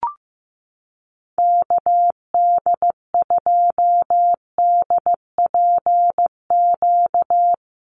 Morse code, heard as a keyed tone, about 15 wpm, 700 Hz. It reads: KD2DPQ
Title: USA Call Signs Repeated 2x 15wpm